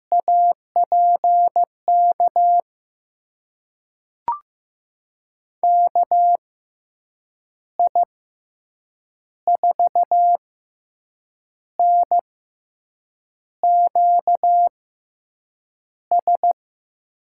Morse code, heard as a keyed tone, about 15 wpm, spaced as for 6 wpm, 700 Hz. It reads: APK KI4NQS
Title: USA Call Signs - 6x Character Spacing - 15wpm